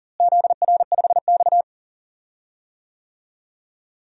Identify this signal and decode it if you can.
Morse code, heard as a keyed tone, about 40 wpm, 700 Hz. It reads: ZR5X
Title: DXCC-100 Call Signs - 4 Characters - Repeated 3x - 40wpm